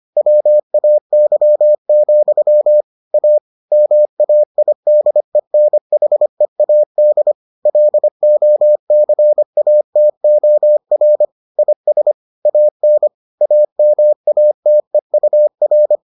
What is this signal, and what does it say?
Morse code, heard as a keyed tone, about 25 wpm, 600 Hz. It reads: WAY, A MAIDENHEAD LOCATOR IS AN AMATEUR